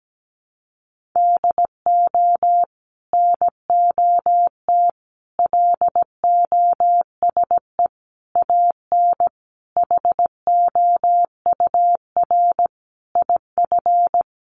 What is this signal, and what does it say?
Morse code, heard as a keyed tone, about 17 wpm, 700 Hz. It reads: DO NOT LOSE AN HOUR IF
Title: Sentences from Top 1100 Words 17wpm